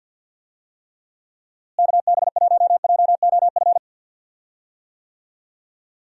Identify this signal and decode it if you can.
Morse code, heard as a keyed tone, about 47 wpm, 700 Hz. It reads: KB1JOF
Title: USA Call Signs - 6 Characters - Repeated 2x - 50wpm